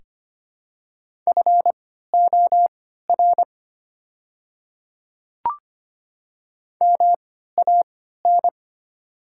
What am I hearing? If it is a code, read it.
Morse code, heard as a keyed tone, about 25 wpm, spaced as for 14 wpm, 700 Hz. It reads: FOR MAN
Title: 3x Character Spacing - Top 300 Words - 25wpm